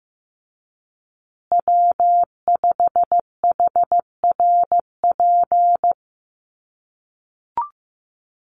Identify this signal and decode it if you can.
Morse code, heard as a keyed tone, about 15 wpm, 700 Hz. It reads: W5HRP
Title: USA Call Signs Repeated 2x 15wpm